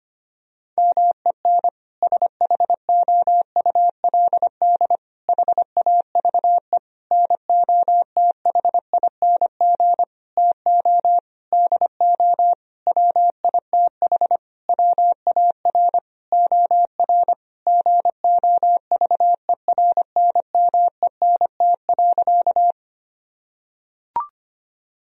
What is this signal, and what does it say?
Morse code, heard as a keyed tone, about 25 wpm, 700 Hz. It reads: MEN SHOULD HAVE NOTHING TO DO WITH WAR OR GOVERNMENT.
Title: Sentences from Top 1000 Words - Review Part 3 - 25wpm